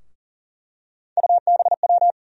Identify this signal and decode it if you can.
Morse code, heard as a keyed tone, about 40 wpm, 700 Hz. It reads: UBW
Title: USA Call Sign Postfixes - 3 Letters 40wpm